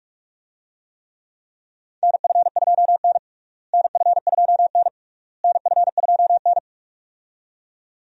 Morse code, heard as a keyed tone, about 47 wpm, 700 Hz. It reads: NF2N NF2N NF2N
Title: USA Call Signs - 4 Characters - Repeated 3x - 45wpm